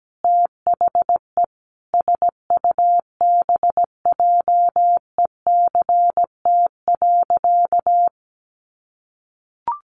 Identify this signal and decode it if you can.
Morse code, heard as a keyed tone, about 17 wpm, 700 Hz. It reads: THE SUBJECT.
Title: Sentences from Top 800 Words 17wpm